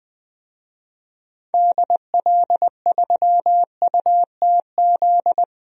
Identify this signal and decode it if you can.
Morse code, heard as a keyed tone, about 20 wpm, 700 Hz. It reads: DL3UTZ
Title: DXCC-100 Call Signs - 6 Characters - Repeated 3x - 20wpm